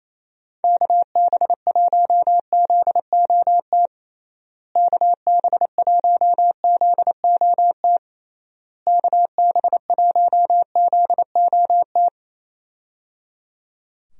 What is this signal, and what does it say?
Morse code, heard as a keyed tone, about 28 wpm, 700 Hz. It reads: KB1ZOT KB1ZOT KB1ZOT
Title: USA Call Signs - 6 Characters - Repeated 3x - 28wpm